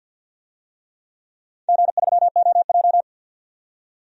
Morse code, heard as a keyed tone, about 48 wpm, 700 Hz. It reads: G3OJ
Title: DXCC-100 Call Signs - 4 Characters - 50wpm